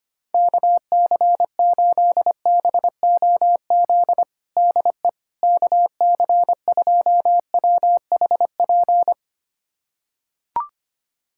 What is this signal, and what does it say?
Morse code, heard as a keyed tone, about 25 wpm, 700 Hz. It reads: KC8BOZ DE KC2WHP
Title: USA Callsign DE Callsign - 6 Characters - 25wpm